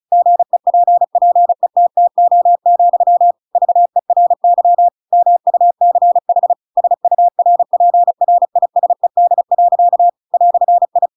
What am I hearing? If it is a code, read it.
Morse code, heard as a keyed tone, about 35 wpm, 700 Hz. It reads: GEPPETTO, VERY MUCH SURPRISED. "I
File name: Pinocchio / Pinocchio ch7 35 35 ik1hge